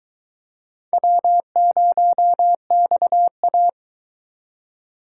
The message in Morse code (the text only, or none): W0XA